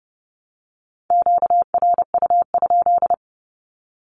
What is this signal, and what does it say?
Morse code, heard as a keyed tone, about 30 wpm, 700 Hz. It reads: QRU?
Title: Q-Codes Repeated 3x 30wpm